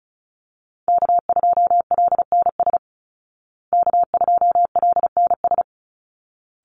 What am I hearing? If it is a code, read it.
Morse code, heard as a keyed tone, about 34 wpm, 700 Hz. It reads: K2LNS K2LNS
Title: Call Signs Repeated 3x - 35wpm